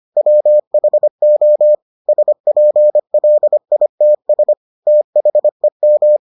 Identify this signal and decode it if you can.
Morse code, heard as a keyed tone, about 25 wpm, 600 Hz. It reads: WHO SPLITS THEM